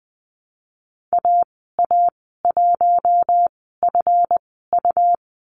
Morse code, heard as a keyed tone, about 20 wpm, spaced as for 14 wpm, 700 Hz. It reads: AA1FU
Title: 179 Lesson 29g - Slash - Call Signs 20wpm